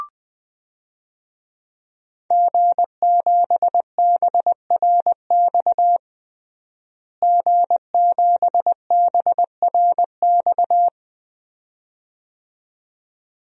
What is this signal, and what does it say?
Morse code, heard as a keyed tone, about 21 wpm, 700 Hz. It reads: G7BRX G7BRX
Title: DXCC-100 Call Signs - 5 Characters - Repeated 2x - 20wpm